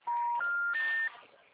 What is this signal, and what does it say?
Special information tone, with noise.
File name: out- -2009